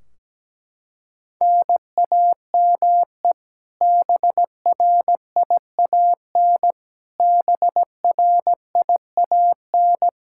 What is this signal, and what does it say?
Morse code, heard as a keyed tone, about 17 wpm, 700 Hz. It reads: NAME BRIAN BRIAN
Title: K1USN SST QSO Practice 17wpm